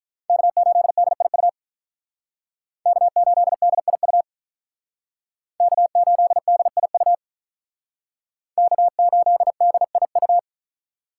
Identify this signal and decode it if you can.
Morse code, heard as a keyed tone, about 46 wpm, 700 Hz. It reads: K8DIU K8DIU K8DIU K8DIU